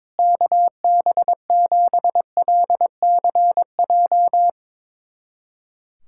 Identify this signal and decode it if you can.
Morse code, heard as a keyed tone, about 22 wpm, 700 Hz. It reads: KB7LCJ
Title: USA Callsign DE Callsign 22wpm